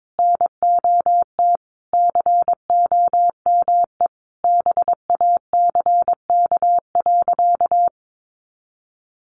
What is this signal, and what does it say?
Morse code, heard as a keyed tone, about 22 wpm, 700 Hz. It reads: NOT COME BACK.
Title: Sentences from Top 1300 Words 22wpm